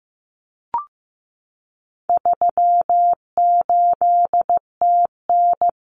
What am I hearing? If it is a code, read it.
Morse code, heard as a keyed tone, about 15 wpm, 700 Hz. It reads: 38TN
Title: Common Cut Numbers - 4-Digit 15wpm